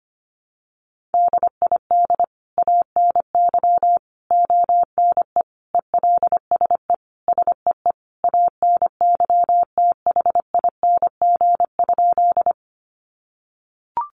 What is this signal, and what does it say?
Morse code, heard as a keyed tone, about 25 wpm, 700 Hz. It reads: DID ANY ONE ELSE SEE ANYTHING?
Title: Sentences from Top 1000 Words 25wpm